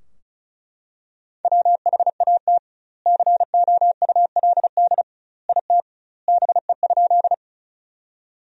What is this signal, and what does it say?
Morse code, heard as a keyed tone, about 34 wpm, 700 Hz. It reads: WHAT COULD IT BE?
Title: 2x Word-Spacing - Sentences from Top 100 Words - 35wpm